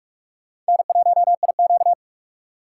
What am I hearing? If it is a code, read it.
Morse code, heard as a keyed tone, about 62 wpm, spaced as for 53 wpm, 700 Hz. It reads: N1IQ
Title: Speed Racing - USA Call Signs 45wpm